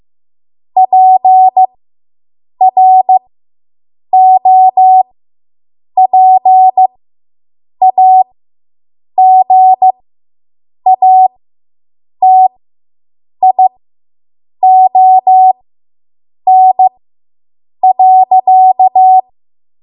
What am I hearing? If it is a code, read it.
Morse code, heard as a keyed tone, about 15 wpm, spaced as for 7 wpm, 750 Hz. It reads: PROPAGATION.